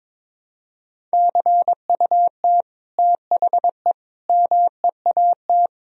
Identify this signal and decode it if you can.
Morse code, heard as a keyed tone, about 22 wpm, 700 Hz. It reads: CUT THE MEAT
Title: Sentences from Top 1000 Words - Review Part 3 - 22wpm